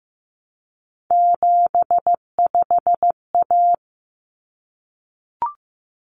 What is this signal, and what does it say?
Morse code, heard as a keyed tone, about 15 wpm, 700 Hz. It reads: 75A
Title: Common Cut Numbers - 3-Digit 15wpm